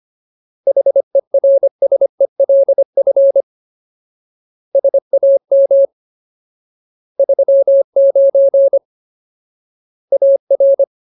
Morse code, heard as a keyed tone, about 25 wpm, 550 Hz. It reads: HERSELF SAM 39 AR
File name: Ultimate Word List / 25WPM Ultimate